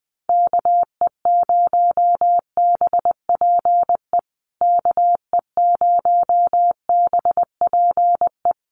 Morse code, heard as a keyed tone, about 20 wpm, 700 Hz. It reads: KE0BPE KE0BPE
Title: USA Call Signs Repeated 3x 20wpm